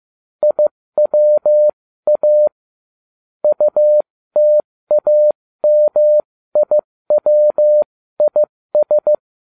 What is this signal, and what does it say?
Morse code, heard as a keyed tone, about 15 wpm, spaced as for 13 wpm, 600 Hz. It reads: IWA UTAMIWIS